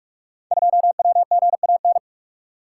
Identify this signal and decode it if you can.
Morse code, heard as a keyed tone, about 44 wpm, 700 Hz. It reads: 2WGAN
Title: Code Groups - 5 Letter:Numbers - 45wpm